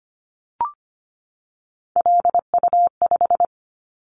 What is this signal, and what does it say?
Morse code, heard as a keyed tone, about 25 wpm, 700 Hz. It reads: LU5
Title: Callsign Prefixes for Top 100 DXCC - 25wpm